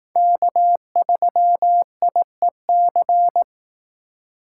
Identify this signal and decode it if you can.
Morse code, heard as a keyed tone, about 18 wpm, 700 Hz. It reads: K3IEC
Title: Speed Racing - USA 1x3 Call Signs 15wpm